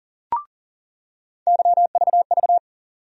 Morse code, heard as a keyed tone, about 40 wpm, 700 Hz. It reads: YVV
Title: USA Call Sign Postfixes - 3 Letters 40wpm